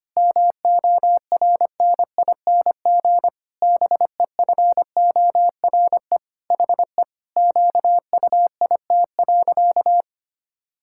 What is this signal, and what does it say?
Morse code, heard as a keyed tone, about 25 wpm, 700 Hz. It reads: MORNING BEFORE HE QUIT.
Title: Pangram Challenge v2 - 25wpm